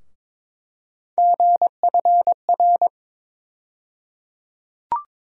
Morse code, heard as a keyed tone, about 22 wpm, 700 Hz. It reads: GFR
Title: USA Call Sign Postfixes - 3 Letters 22wpm